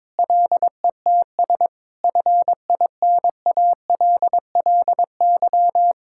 Morse code, heard as a keyed tone, about 22 wpm, 700 Hz. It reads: LETS FINALLY
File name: practice / 22wpm / Sets of 8 Words - 22wpm